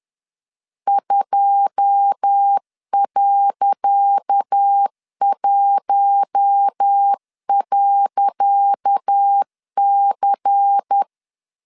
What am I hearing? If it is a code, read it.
Morse code, heard as a keyed tone, about 10 wpm, 800 Hz. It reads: 2.1.C
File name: LEZIONI / 10 SIGNS / 10C DOT 10wpm 2940